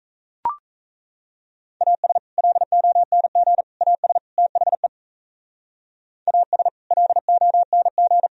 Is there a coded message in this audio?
Yes: Morse code from 1.8 to 8.3 s, heard as a keyed tone, about 35 wpm, 700 Hz. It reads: AS LONG AS THE AS LONG